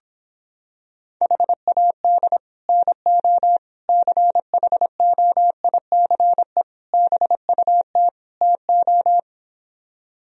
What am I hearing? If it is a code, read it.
Morse code, heard as a keyed tone, about 26 wpm, 700 Hz. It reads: HAD NO CHOICE BUT TO